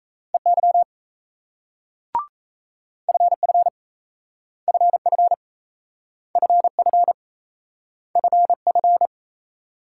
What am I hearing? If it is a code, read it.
Morse code, heard as a keyed tone, about 38 wpm, 700 Hz. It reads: EY FF FF FF FF